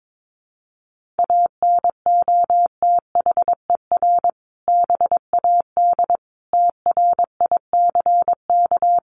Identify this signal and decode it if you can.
Morse code, heard as a keyed tone, about 22 wpm, 700 Hz. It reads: ANOTHER BAD TRICK
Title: Sentences from Top 1500 Words 22wpm